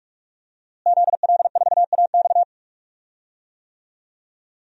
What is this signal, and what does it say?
Morse code, heard as a keyed tone, about 46 wpm, 700 Hz. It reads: ZL4AX
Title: DXCC-100 Call Signs - 5 Characters - Repeated 3x - 45wpm